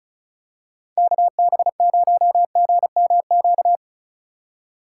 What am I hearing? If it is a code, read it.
Morse code, heard as a keyed tone, about 36 wpm, 700 Hz. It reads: KB0GMQ
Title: USA Call Signs - 6 Characters - 2x Character Spacing - 35wpm